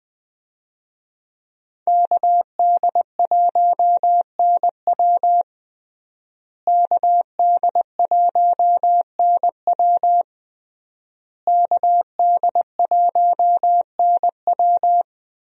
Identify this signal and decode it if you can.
Morse code, heard as a keyed tone, about 21 wpm, 700 Hz. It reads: KD1NW KD1NW KD1NW
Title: USA Call Signs - 5 Characters - Repeated 3x - 20wpm